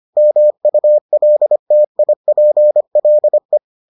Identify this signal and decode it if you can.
Morse code, heard as a keyed tone, about 25 wpm, 600 Hz. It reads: MULTIPLE